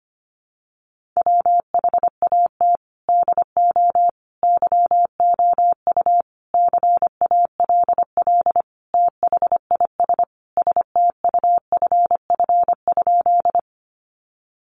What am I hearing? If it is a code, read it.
Morse code, heard as a keyed tone, about 25 wpm, 700 Hz. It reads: WHAT DO YOU CALL THIS STUFF?
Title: Sentences from Top 1400 Words 25wpm